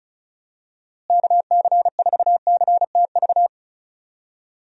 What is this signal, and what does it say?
Morse code, heard as a keyed tone, about 34 wpm, 700 Hz. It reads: KC4CTV
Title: USA Call Signs - 6 Characters - Repeated 2x - 35wpm